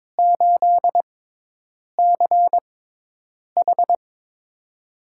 Morse code, heard as a keyed tone, about 22 wpm, spaced as for 8 wpm, 700 Hz. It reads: 8CH
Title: USA Call Signs - 4 Characters - 6x Character Spacing - 22wpm